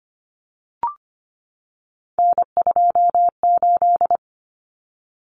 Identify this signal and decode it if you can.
Morse code, heard as a keyed tone, about 25 wpm, 700 Hz. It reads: N28
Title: Common Cut Numbers - 3-Digit 25wpm